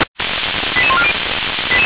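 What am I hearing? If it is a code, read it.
Five-tone selective call (ZVEI), heard as quick runs of five tones, tones of about 70 ms. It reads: 88140
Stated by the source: ZVEI five-tone selective call received by SDR, recorded by fire brigade Feuerwehr Orlamünde.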